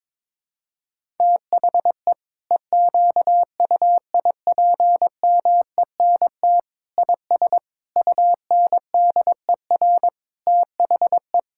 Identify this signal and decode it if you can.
Morse code, heard as a keyed tone, about 22 wpm, 700 Hz. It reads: THE EQUIPMENT IS UNDER THE
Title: Sentences from Top 1000 Words - Review Part 5 - 22wpm